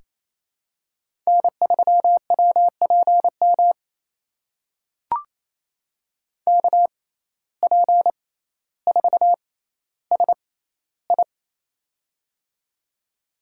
Morse code, heard as a keyed tone, about 28 wpm, spaced as for 14 wpm, 700 Hz. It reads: N3WPM KP4SI
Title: USA Call Signs - 5 Characters - 6x Character Spacing - 28wpm